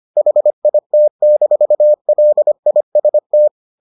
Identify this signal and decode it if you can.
Morse code, heard as a keyed tone, about 25 wpm, 600 Hz. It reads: HIT-LIST